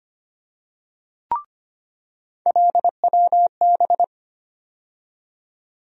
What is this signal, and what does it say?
Morse code, heard as a keyed tone, about 25 wpm, 700 Hz. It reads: LWB
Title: USA Call Sign Postfixes - 3 Letters 25wpm